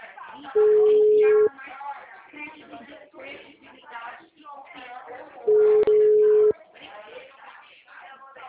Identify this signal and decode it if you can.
Ringback tone (Europe, 425 Hz), with speech.
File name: out- -2026- - - .69678